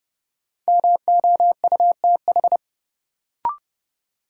Morse code, heard as a keyed tone, about 29 wpm, 700 Hz. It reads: MOUTH
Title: Sentences from Top 1500 Words - New Word Drill Alt - 30wpm